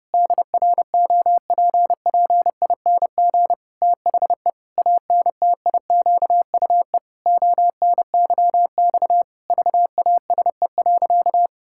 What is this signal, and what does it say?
Morse code, heard as a keyed tone, about 30 wpm, 700 Hz. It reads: DROPPING THE ANTIQUE ONYX VASE.
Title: Pangram Challenge v2 - 30wpm